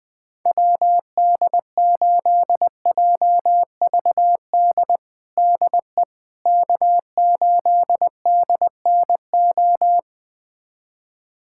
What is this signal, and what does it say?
Morse code, heard as a keyed tone, about 21 wpm, 700 Hz. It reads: WD8JVD DE K8DNO
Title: USA Callsign DE Callsign 20wpm